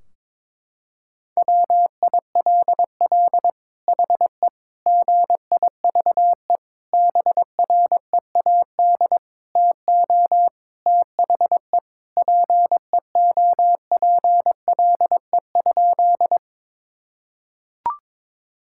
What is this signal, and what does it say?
Morse code, heard as a keyed tone, about 22 wpm, 700 Hz. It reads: WILL HE GIVE BREAD TO THE PEOPLE?
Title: Sentences from Top 1600 Words 22wpm